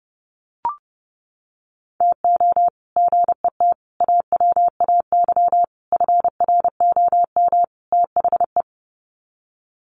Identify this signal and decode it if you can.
Morse code, heard as a keyed tone, about 30 wpm, 700 Hz. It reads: TO GET AWAY FROM THE
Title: Sets of 5 Words Encore2 30wpm